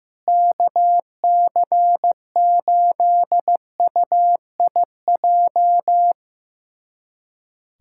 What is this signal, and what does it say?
Morse code, heard as a keyed tone, about 15 wpm, 700 Hz. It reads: KC8UIJ
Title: USA Callsign DE Callsign - 6 Characters - 15wpm